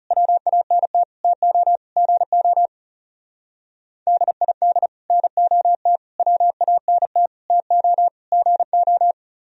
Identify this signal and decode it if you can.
Morse code, heard as a keyed tone, about 38 wpm, 700 Hz. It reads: WANT TO GO DID NOT WANT TO GO
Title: Speed-Racing - Sets of 5 Words 30wpm